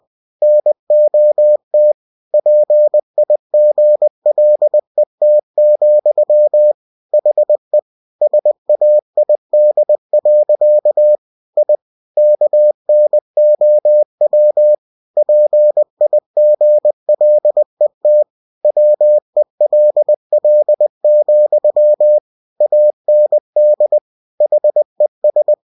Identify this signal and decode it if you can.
Morse code, heard as a keyed tone, about 20 wpm, 600 Hz. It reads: NOT PIGLET, HE SAID. I KNOW PIGLET WELL, AND HES